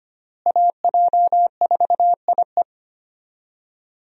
Morse code, heard as a keyed tone, about 25 wpm, 700 Hz. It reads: AJ4IE
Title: USA Call Signs - 5 Characters - Repeated 3x - 25wpm